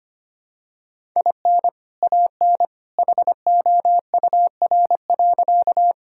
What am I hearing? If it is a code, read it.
Morse code, heard as a keyed tone, about 25 wpm, 700 Hz. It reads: IN AN HOUR.
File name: practice / 25wpm / Sentences with 2-Word Phrases - 25wpm